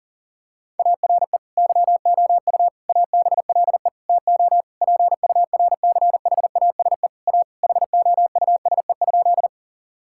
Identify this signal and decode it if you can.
Morse code, heard as a keyed tone, about 37 wpm, 700 Hz. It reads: ARE YOU ABLE TO PURCHASE A HOUSE?
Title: Sentences from Top 1000 Words - Review Part 5 - 40wpm